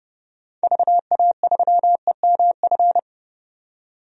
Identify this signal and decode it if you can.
Morse code, heard as a keyed tone, about 30 wpm, 690 Hz. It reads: VA3EMF
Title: DXCC-100 Call Signs - 6 Characters - 30wpm